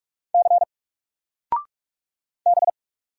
Morse code, heard as a keyed tone, about 44 wpm, 700 Hz. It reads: C B
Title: Single Letter-Number 45wpm